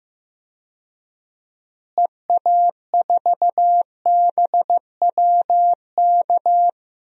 Morse code, heard as a keyed tone, about 15 wpm, 700 Hz. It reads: EA4BWK
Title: DXCC-100 Call Signs - 6 Characters - Repeated 2x - 15wpm